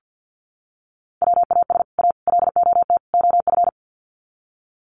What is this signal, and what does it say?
Morse code, heard as a keyed tone, about 50 wpm, 700 Hz. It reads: WAS A LOT OF
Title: Sets of 4 Words 50wpm